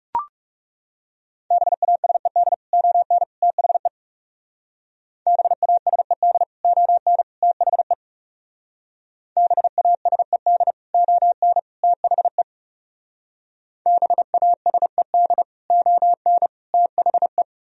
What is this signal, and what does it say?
Morse code, heard as a keyed tone, about 39 wpm, 700 Hz. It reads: BASED ON THE BASED ON THE BASED ON THE BASED ON THE